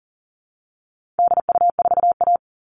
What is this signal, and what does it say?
Morse code, heard as a keyed tone, about 40 wpm, 700 Hz. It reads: DU4A
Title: All Cut Numbers - 4-Digit 40wpm